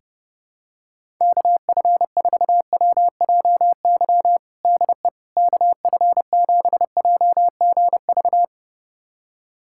Morse code, heard as a keyed tone, about 32 wpm, 700 Hz. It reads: KF4WJY DE KF7JGV
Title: USA Callsign DE Callsign 30wpm